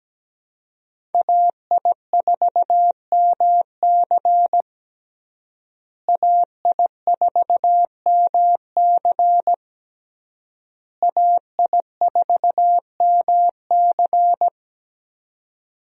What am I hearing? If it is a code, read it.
Morse code, heard as a keyed tone, about 17 wpm, 700 Hz. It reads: AI4MC AI4MC AI4MC
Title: USA Call Signs - 5 Characters - Repeated 3x - 17wpm